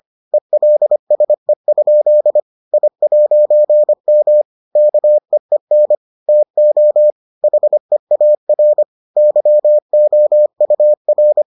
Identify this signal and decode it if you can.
Morse code, heard as a keyed tone, about 25 wpm, 600 Hz. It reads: ELSE? I'M KEEN TO HEAR YOUR